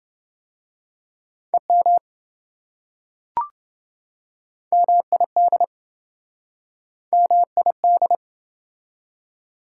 Morse code, heard as a keyed tone, about 27 wpm, 700 Hz. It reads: EM MID MID